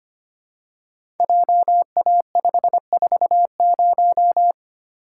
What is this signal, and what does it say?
Morse code, heard as a keyed tone, about 25 wpm, 700 Hz. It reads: JA540
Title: Code Groups - 5 Letter:Numbers - 25wpm